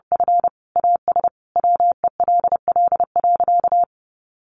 Morse code, heard as a keyed tone, about 30 wpm, 700 Hz. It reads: F AS WELL.
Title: Sentences from Top 1300 Words 30wpm